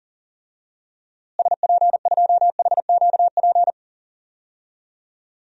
Morse code, heard as a keyed tone, about 40 wpm, 700 Hz. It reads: SP2HQP
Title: DXCC-100 Call Signs - 6 Characters - Repeated 3x - 40wpm